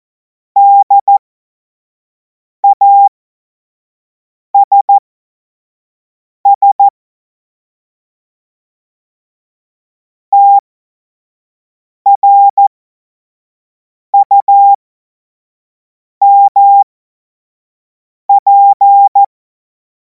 Morse code, heard as a keyed tone, about 14 wpm, spaced as for 5 wpm, 800 Hz. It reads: DASS TRUMP
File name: Morsekurs / Witze / Witz 12. Mai 2026 25 F70